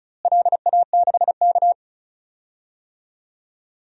Morse code, heard as a keyed tone, about 36 wpm, 690 Hz. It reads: LA6K
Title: DXCC-100 Call Signs - 4 Characters - Repeated 3x - 35wpm